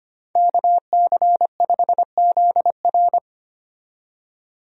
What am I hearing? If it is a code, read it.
Morse code, heard as a keyed tone, about 25 wpm, 700 Hz. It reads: KC5ZR
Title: USA Call Signs - 5 Characters - 25wpm